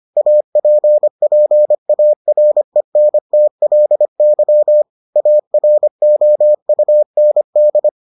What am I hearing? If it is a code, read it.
Morse code, heard as a keyed tone, about 25 wpm, 600 Hz. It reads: APPARENTLY AROUND